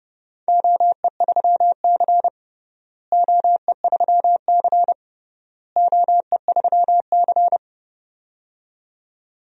Morse code, heard as a keyed tone, about 30 wpm, 700 Hz. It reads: OE3C OE3C OE3C
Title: DXCC-100 Call Signs - 4 Characters - Repeated 3x - 30wpm